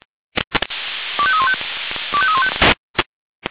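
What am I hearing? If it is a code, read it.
Five-tone selective call (ZVEI), heard as quick runs of five tones, tones of about 70 ms. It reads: 25315 25315
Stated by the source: ZVEI five-tone call 25314 received by SDR.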